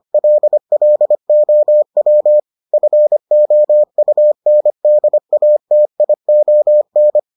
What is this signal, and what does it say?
Morse code, heard as a keyed tone, about 25 wpm, 600 Hz. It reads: LLOW FOUNDATION